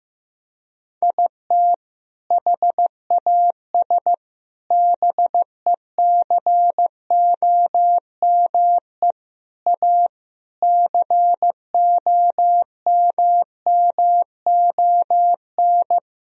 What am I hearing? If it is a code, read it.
Morse code, heard as a keyed tone, about 15 wpm, 700 Hz. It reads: IT HAS BECOME A COMMON
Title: Sentences from Top 1000 Words - Review Part 3 - 15wpm